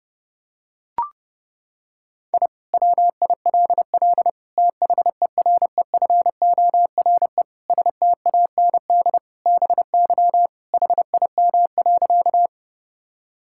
Morse code, heard as a keyed tone, about 29 wpm, 700 Hz. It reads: I WILL THEREFORE STAND BY HIM.
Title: Sentences from Top 1000 Words - Review Part 5 - 30wpm